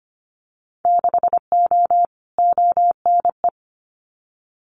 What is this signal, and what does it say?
Morse code, heard as a keyed tone, about 25 wpm, 700 Hz. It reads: 6O ONE
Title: Winter Field Day Exchanges Encore 25wpm